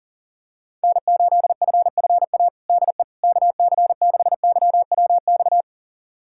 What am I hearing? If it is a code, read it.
Morse code, heard as a keyed tone, about 40 wpm, 700 Hz. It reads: N8FFA DE KC6YWX
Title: USA Callsign DE Callsign 40wpm